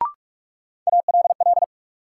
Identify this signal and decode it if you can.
Morse code, heard as a keyed tone, about 47 wpm, 700 Hz. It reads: ALL
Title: Sentences from Top 100 Words - New Word Drill - 45wpm